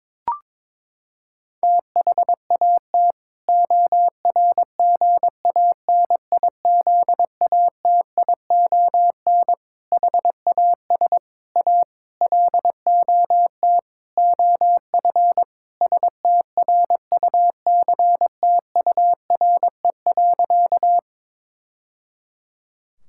Morse code, heard as a keyed tone, about 22 wpm, 700 Hz. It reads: THAT ORGANIZATION HAS A LOT OF STRUCTURE.
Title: Sentences from Top 1000 Words - Review Part 6 - 22wpm